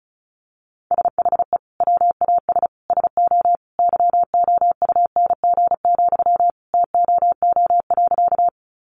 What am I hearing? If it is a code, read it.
Morse code, heard as a keyed tone, about 34 wpm, 700 Hz. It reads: SHE WAS SO YOUNG, TOO.
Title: Sentences from Top 400 Words 35wpm